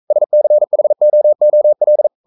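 Morse code, heard as a keyed tone, about 42 wpm, 600 Hz. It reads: SCHOOL
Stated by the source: Morse code practice at 41 wpm, from CW med Gnister.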